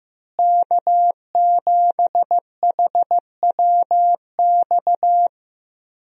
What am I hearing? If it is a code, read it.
Morse code, heard as a keyed tone, about 16 wpm, 700 Hz. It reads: K7HWX
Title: USA Call Signs - 5 Characters - Repeated 3x - 15wpm